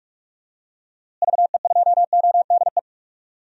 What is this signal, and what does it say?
Morse code, heard as a keyed tone, about 42 wpm, 700 Hz. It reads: VE2ODE
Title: DXCC-100 Call Signs - 6 Characters - Repeated 2x - 45wpm